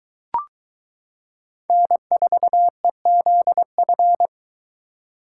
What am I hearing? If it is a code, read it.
Morse code, heard as a keyed tone, about 22 wpm, 700 Hz. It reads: N4EZF